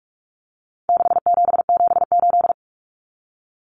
Morse code, heard as a keyed tone, about 44 wpm, 700 Hz. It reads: 6778
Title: Numbers - 4-Digit 45wpm